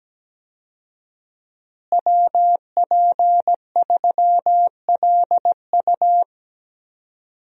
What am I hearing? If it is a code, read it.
Morse code, heard as a keyed tone, about 17 wpm, 700 Hz. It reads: WP3LU
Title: USA Call Signs - 5 Characters - Repeated 2x - 17wpm